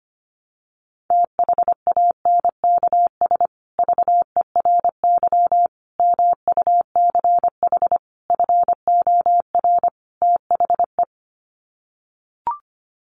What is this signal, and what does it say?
Morse code, heard as a keyed tone, about 25 wpm, 700 Hz. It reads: THANKS VERY MUCH FOR THE
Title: Sets of 5 Words Encore2 25wpm